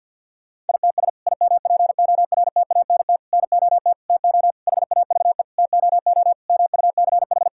Morse code, heard as a keyed tone, about 50 wpm, 700 Hz. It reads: ITS IMPORTANT NOT TO HAVE TOO MUCH
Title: Sentences from Top 1000 Words - Review Part 5 - 50wpm